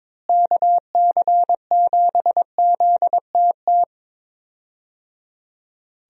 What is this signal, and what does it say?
Morse code, heard as a keyed tone, about 22 wpm, 700 Hz. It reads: KC7ZTT
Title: USA Call Signs - 6 Characters - Repeated 3x - 22wpm